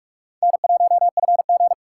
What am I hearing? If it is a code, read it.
Morse code, heard as a keyed tone, about 46 wpm, 700 Hz. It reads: N1FG
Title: USA Call Signs - 4 Characters - 6x Character Spacing - 45wpm